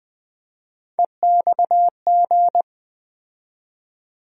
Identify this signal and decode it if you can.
Morse code, heard as a keyed tone, about 20 wpm, 700 Hz. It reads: EXG
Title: USA Call Sign Postfixes - 3 Letters 20wpm